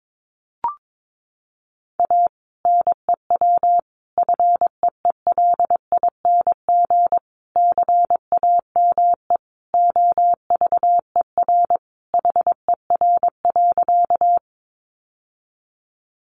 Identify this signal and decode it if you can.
Morse code, heard as a keyed tone, about 22 wpm, 700 Hz. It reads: A NEW FEELING CAME OVER HER.
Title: Sentences from Top 1200 Words 22wpm